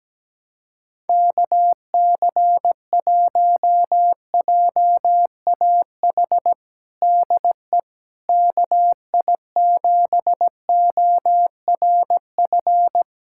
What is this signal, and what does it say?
Morse code, heard as a keyed tone, about 17 wpm, 700 Hz. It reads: KC1JAH DE KI7ORF
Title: USA Callsign DE Callsign 17wpm